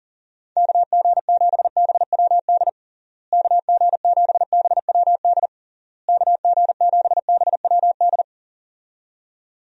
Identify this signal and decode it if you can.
Morse code, heard as a keyed tone, about 37 wpm, 700 Hz. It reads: KG7BWD KG7BWD KG7BWD
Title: USA Call Signs - 6 Characters - Repeated 3x - 40wpm